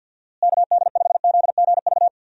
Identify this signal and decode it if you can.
Morse code, heard as a keyed tone, about 52 wpm, 700 Hz. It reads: KD5ZGV
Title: USA Call Signs - 6 Characters - Repeated 2x - 50wpm